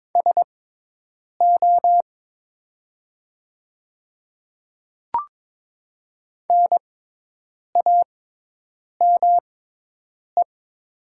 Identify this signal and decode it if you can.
Morse code, heard as a keyed tone, about 22 wpm, spaced as for 8 wpm, 700 Hz. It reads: SO NAME
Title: 6x Character Spacing - Top 300 Words - 22wpm